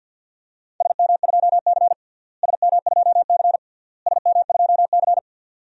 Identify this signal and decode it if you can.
Morse code, heard as a keyed tone, about 52 wpm, 700 Hz. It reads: SM2C SM2C SM2C
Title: DXCC-100 Call Signs - 4 Characters - Repeated 3x - 50wpm